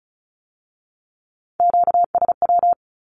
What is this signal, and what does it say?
Morse code, heard as a keyed tone, about 36 wpm, 700 Hz. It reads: QSW
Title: Q-Codes with Spoken Meaning - 35wpm